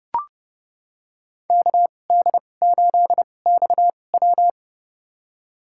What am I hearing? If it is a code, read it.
Morse code, heard as a keyed tone, about 29 wpm, spaced as for 21 wpm, 700 Hz. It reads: KD8XW
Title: USA Call Signs - 5 Characters - 2x Character Spacing - 30wpm